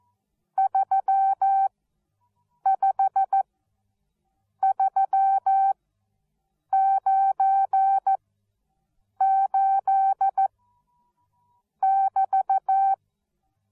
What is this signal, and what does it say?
Morse code, heard as a keyed tone, about 14 wpm, spaced as for 6 wpm, 780 Hz. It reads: 35398<BT>
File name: Onlinelehrgang / morsen / DJ4UF Morsen13